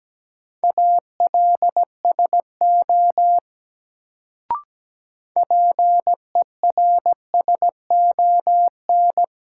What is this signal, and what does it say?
Morse code, heard as a keyed tone, about 17 wpm, 700 Hz. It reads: ALSO PERSON
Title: Sentences from Top 400 Words - New Word Drill - 17wpm